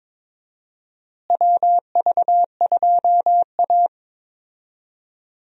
Morse code, heard as a keyed tone, about 22 wpm, 700 Hz. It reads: WV2A
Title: USA Call Signs - 4 Characters - Repeated 2x - 22wpm